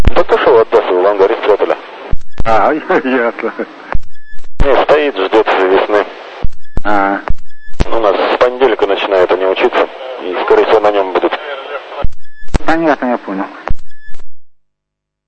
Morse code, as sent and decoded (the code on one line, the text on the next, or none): none